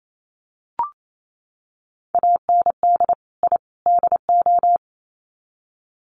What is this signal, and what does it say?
Morse code, heard as a keyed tone, about 28 wpm, 700 Hz. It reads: AND I DO
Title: Sets of 3 Words 28wpm